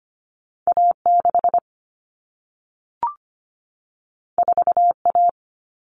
Morse code, heard as a keyed tone, about 25 wpm, 700 Hz. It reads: A6 4A
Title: All Cut Numbers - 2-Digit 25wpm